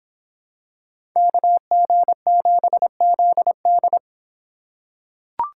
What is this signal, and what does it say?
Morse code, heard as a keyed tone, about 26 wpm, 700 Hz. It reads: KG7ZD